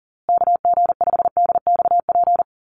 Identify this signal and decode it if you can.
Morse code, heard as a keyed tone, about 38 wpm, 700 Hz. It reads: KG5DXP
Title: USA Call Signs Repeated 3x 40wpm